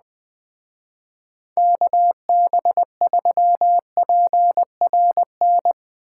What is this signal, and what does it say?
Morse code, heard as a keyed tone, about 20 wpm, 700 Hz. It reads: KB3PRN
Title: Speed Racing - USA 2x3 Call Signs 17wpm